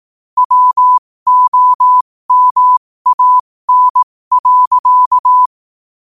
Morse code, heard as a keyed tone, about 18 wpm, spaced as for 15 wpm, 1000 Hz. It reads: WOMAN.
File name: qotdfiles / 15wpm